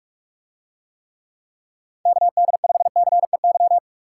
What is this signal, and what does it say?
Morse code, heard as a keyed tone, about 46 wpm, 700 Hz. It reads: KD5CEY